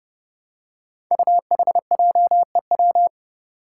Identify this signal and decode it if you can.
Morse code, heard as a keyed tone, about 32 wpm, 700 Hz. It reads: UHJEW
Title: Five Challenging Letter-Number Combinations 30wpm